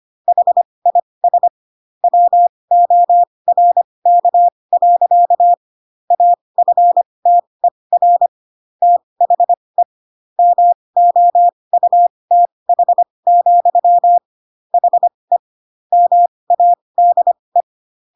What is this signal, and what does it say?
Morse code, heard as a keyed tone, about 25 wpm, spaced as for 20 wpm, 700 Hz. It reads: HIS WORK. AFTER THE MOUTH, HE MADE
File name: Pinocchio / Pinocchio ch3 15 25 ik1hge